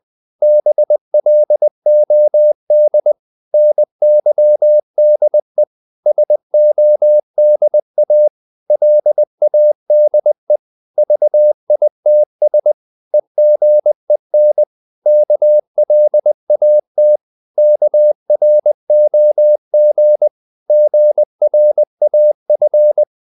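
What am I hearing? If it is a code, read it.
Morse code, heard as a keyed tone, about 20 wpm, 600 Hz. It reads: BLOD NYDE SODA LADE VITS EGEN KLAT KROG GRAF
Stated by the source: Morse code practice of short Danish words at 20 wpm, from CW med Gnister.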